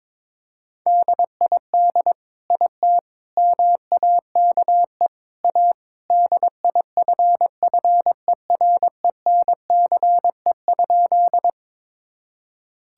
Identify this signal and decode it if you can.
Morse code, heard as a keyed tone, about 22 wpm, 700 Hz. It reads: DID IT MAKE A DIFFERENCE?
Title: Sentences from Top 1000 Words - Review Part 5 - 22wpm